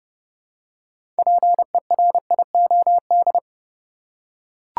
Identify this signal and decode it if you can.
Morse code, heard as a keyed tone, about 30 wpm, 700 Hz. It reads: PERIOD
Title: Six Letter Words - 30wpm